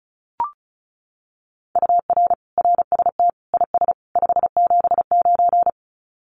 Morse code, heard as a keyed tone, about 34 wpm, 700 Hz. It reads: UR RST IS 579
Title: Common QSO Phrases 35wpm